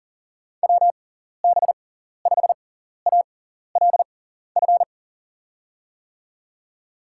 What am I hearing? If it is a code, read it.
Morse code, heard as a keyed tone, about 37 wpm, spaced as for 14 wpm, 700 Hz. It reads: WB5ALF
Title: USA Call Signs - 6 Characters - 6x Character Spacing - 40wpm